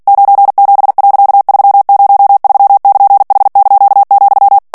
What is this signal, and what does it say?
Morse code, heard as a keyed tone, about 48 wpm, 800 Hz. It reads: 97.303<KN>H),
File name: ANGLAIS texte clair 3 vitesse 45 mots minute